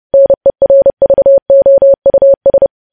Morse code, heard as a keyed tone, about 30 wpm, 550 Hz. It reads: NERVOUS
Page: news headlines in morse code at 30 wpm